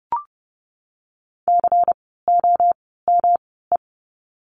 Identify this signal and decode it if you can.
Morse code, heard as a keyed tone, about 30 wpm, spaced as for 17 wpm, 700 Hz. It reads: COME
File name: course / 30:10wpm / 230 Lesson 37f - Letter x - Words 30wpm